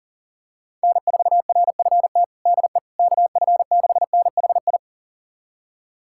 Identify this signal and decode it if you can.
Morse code, heard as a keyed tone, about 37 wpm, 700 Hz. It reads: N4RFT DE KF6NHI
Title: USA Callsign DE Callsign 40wpm